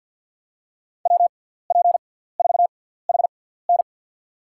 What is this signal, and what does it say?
Morse code, heard as a keyed tone, about 52 wpm, spaced as for 17 wpm, 700 Hz. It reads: WP4HN
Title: USA Call Signs - 5 Characters - 6x Character Spacing - 50wpm